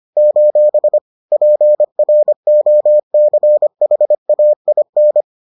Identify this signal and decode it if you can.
Morse code, heard as a keyed tone, about 25 wpm, 600 Hz. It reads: : PROCHAIN
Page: bulletin UFT sous forme texte et en CW